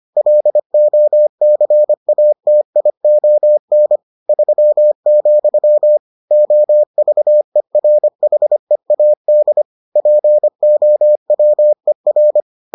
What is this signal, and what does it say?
Morse code, heard as a keyed tone, about 25 wpm, 600 Hz. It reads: LOCATION 3, OVERHEAD POWER